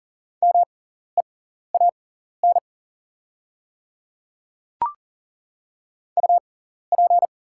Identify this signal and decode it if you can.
Morse code, heard as a keyed tone, about 40 wpm, spaced as for 14 wpm, 690 Hz. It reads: MEAN UP
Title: 6x Character Spacing - Top 200 Words - 40wpm